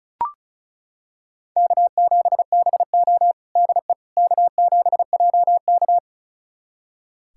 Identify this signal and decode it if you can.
Morse code, heard as a keyed tone, about 36 wpm, 700 Hz. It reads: K7BO DE K7JK
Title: USA Callsign DE Callsign - 4 Characters - 35wpm